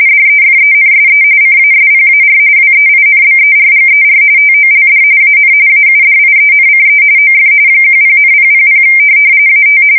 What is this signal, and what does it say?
RTTY, about 45.45 baud. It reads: DEFENSE CASPAR WEINBERGER TOLD AN AUDIENCE OF 10,000 AT
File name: fsk170 / USS-Missouri